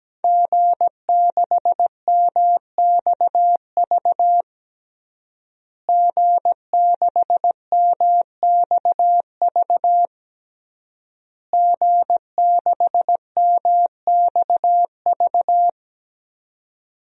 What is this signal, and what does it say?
Morse code, heard as a keyed tone, about 17 wpm, 700 Hz. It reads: G6MXV G6MXV G6MXV
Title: DXCC-100 Call Signs - 5 Characters - Repeated 3x - 17wpm